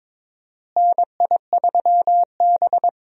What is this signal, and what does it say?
Morse code, heard as a keyed tone, about 22 wpm, 700 Hz. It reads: NI3B
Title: USA Call Signs - 4 Characters - Repeated 2x - 22wpm